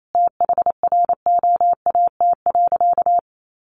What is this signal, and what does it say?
Morse code, heard as a keyed tone, about 28 wpm, 700 Hz. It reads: THROAT.
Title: A Sound Of Thunder 28wpm